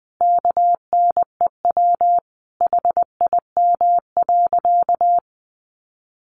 Morse code, heard as a keyed tone, about 20 wpm, 700 Hz. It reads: KNEW HIM.
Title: Sentences from Top 1500 Words 20wpm